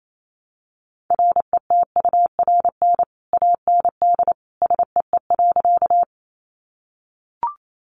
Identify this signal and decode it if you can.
Morse code, heard as a keyed tone, about 28 wpm, 700 Hz. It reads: RETURN AND SEE.
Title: Sentences from Top 800 Words 28wpm